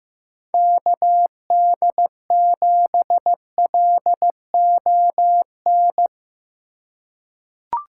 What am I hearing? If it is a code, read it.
Morse code, heard as a keyed tone, about 15 wpm, 700 Hz. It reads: KD7LON
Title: USA Call Signs - 4x Character Spacing - 15wpm